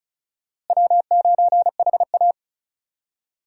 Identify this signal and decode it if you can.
Morse code, heard as a keyed tone, about 34 wpm, 700 Hz. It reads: W9HA
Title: USA Call Signs - 4 Characters - Repeated 3x - 35wpm